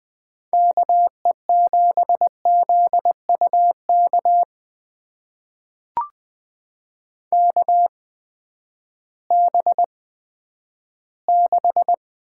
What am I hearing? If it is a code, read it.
Morse code, heard as a keyed tone, about 20 wpm, spaced as for 11 wpm, 700 Hz. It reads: KE7ZUK KB6
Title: USA Call Signs - 6 Characters - 8x Character Spacing - 20wpm